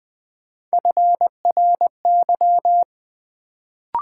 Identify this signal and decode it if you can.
Morse code, heard as a keyed tone, about 20 wpm, 700 Hz. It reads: FRY
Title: Build A Word - 1 Letter At A Time - 3-Letter Words - 20wpm